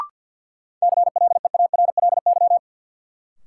Morse code, heard as a keyed tone, about 53 wpm, 700 Hz. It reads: CLEARLY
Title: Sentences from Top 1300 Words - New Word Drill - 50wpm